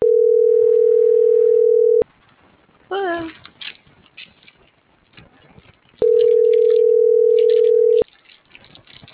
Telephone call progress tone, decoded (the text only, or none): ringback tone